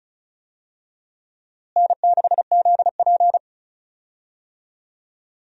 Morse code, heard as a keyed tone, about 36 wpm, 700 Hz. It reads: N6ZP
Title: USA Call Signs - 4 Characters - Repeated 3x - 35wpm